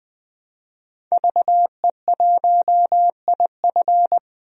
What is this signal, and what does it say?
Morse code, heard as a keyed tone, about 20 wpm, 700 Hz. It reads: VE1IF
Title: DXCC-100 Call Signs - 5 Characters - 20wpm